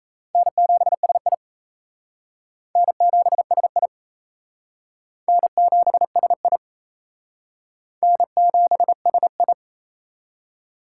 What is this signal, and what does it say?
Morse code, heard as a keyed tone, about 34 wpm, 700 Hz. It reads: N7SI N7SI N7SI N7SI
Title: Speed Racing - USA 1x2 Call Signs 28wpm